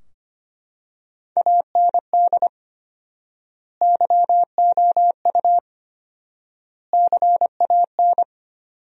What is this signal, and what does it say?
Morse code, heard as a keyed tone, about 25 wpm, 700 Hz. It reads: AND YOU CAN
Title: 4x Word-Spacing - Sentences from Top 500 Words - 25wpm